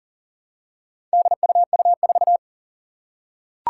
Morse code, heard as a keyed tone, about 40 wpm, 700 Hz. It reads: DUU4
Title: Four Challenging Letter-Number Combinations 40wpm